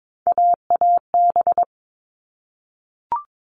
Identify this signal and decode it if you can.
Morse code, heard as a keyed tone, about 22 wpm, 700 Hz. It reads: AAB
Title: All Cut Numbers - 3-Digit 22wpm